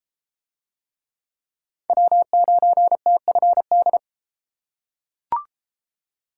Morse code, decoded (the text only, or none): W9TFD